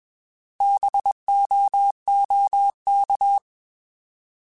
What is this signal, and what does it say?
Morse code, heard as a keyed tone, about 21 wpm, 800 Hz. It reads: BOOK
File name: Alphabetical / Alphabetical mp3s / Bs @20